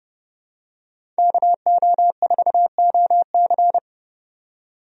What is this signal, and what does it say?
Morse code, heard as a keyed tone, about 30 wpm, 700 Hz. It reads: KO4OC
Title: USA Call Signs - 5 Characters - 30wpm